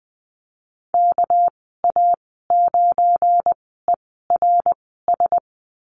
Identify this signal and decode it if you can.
Morse code, heard as a keyed tone, about 19 wpm, spaced as for 14 wpm, 700 Hz. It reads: KA9ERS
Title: 200 Lesson 32g - Letter k - Call Signs 20wpm